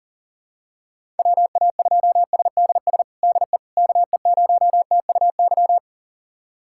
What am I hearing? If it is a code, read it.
Morse code, heard as a keyed tone, about 37 wpm, 700 Hz. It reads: WA2SDS DE KE0TUY
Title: USA Callsign DE Callsign - 6 Characters - 40wpm